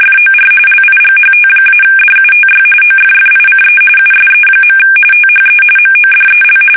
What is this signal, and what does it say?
RTTY, about 45.45 baud. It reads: FOR LIGHT-DUTY OFFICE USE. THE MODEL
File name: fsk850 / teletype-corp